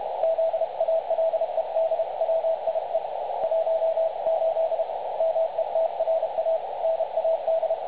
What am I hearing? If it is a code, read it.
Morse code, heard as a keyed tone, about 29 wpm, 660 Hz, with radio noise. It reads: ZALEDWIE 18 METRA NAD
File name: dlazaawansowanych / nagraniazpasma / websdr recording 3565.0kHz